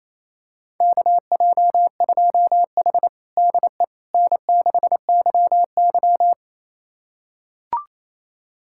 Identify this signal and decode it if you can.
Morse code, heard as a keyed tone, about 28 wpm, 700 Hz. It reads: KJ2H DE N6YY
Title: USA Callsign DE Callsign - 4 Characters - 28wpm